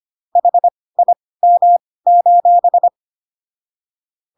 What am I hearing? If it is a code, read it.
Morse code, heard as a keyed tone, about 25 wpm, spaced as for 18 wpm, 700 Hz. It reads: HIM: